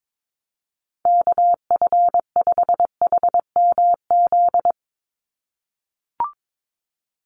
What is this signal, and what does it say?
Morse code, heard as a keyed tone, about 22 wpm, 700 Hz. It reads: KF5HMZ
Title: USA Call Signs 22wpm